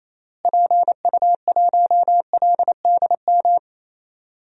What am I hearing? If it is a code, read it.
Morse code, heard as a keyed tone, about 27 wpm, 700 Hz. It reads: PU1LDM
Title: DXCC-100 Call Signs - 6 Characters - Repeated 3x - 28wpm